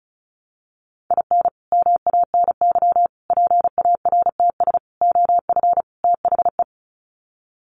Morse code, heard as a keyed tone, about 34 wpm, 700 Hz. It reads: IN MANY PARTS OF THE
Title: Sets of 5 Words Encore 35wpm